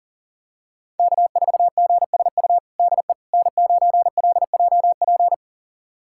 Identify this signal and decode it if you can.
Morse code, heard as a keyed tone, about 40 wpm, 700 Hz. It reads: K4GSU DE N9LJP
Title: USA Callsign DE Callsign 40wpm